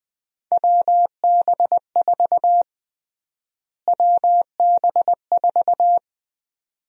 Morse code, heard as a keyed tone, about 20 wpm, 700 Hz. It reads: WB4 WB4
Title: USA Call Sign Prefixes - Repeated 2x - 20wpm